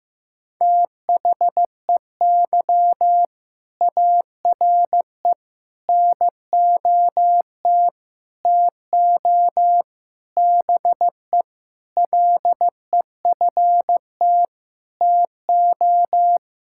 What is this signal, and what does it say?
Morse code, heard as a keyed tone, about 15 wpm, 700 Hz. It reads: THEY ARE NOT TO BE LEFT TO
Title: Sentences from Top 1000 Words - Review Part 5 - 15wpm